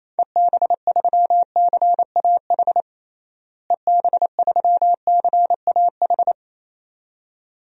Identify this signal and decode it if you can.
Morse code, heard as a keyed tone, about 27 wpm, 700 Hz. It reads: EB3CAH EB3CAH
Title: DXCC-100 Call Signs - 6 Characters - Repeated 2x - 28wpm